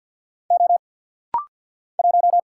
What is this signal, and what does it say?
Morse code, heard as a keyed tone, about 50 wpm, 700 Hz. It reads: O 1
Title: Single Letter-Number Alt - Rapid-Fire 50wpm